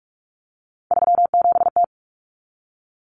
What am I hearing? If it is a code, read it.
Morse code, heard as a keyed tone, about 44 wpm, 700 Hz. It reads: 37T
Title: Common Cut Numbers - 3-Digit 45wpm